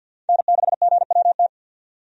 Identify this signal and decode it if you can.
Morse code, heard as a keyed tone, about 48 wpm, 700 Hz. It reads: N6GWT
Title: USA Call Signs - 5 Characters - 50wpm